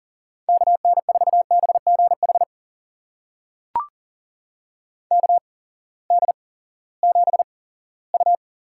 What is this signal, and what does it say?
Morse code, heard as a keyed tone, about 40 wpm, spaced as for 20 wpm, 700 Hz. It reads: KN4BGH KD7U
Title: USA Call Signs - 6 Characters - 8x Character Spacing - 40wpm